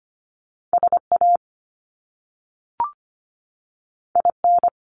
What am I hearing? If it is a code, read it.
Morse code, heard as a keyed tone, about 25 wpm, 700 Hz. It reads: SA IN
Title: Top Two Letter Combinations in Words 25wpm